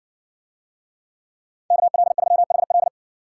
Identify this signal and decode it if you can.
Morse code, heard as a keyed tone, about 53 wpm, 700 Hz. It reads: KL3HL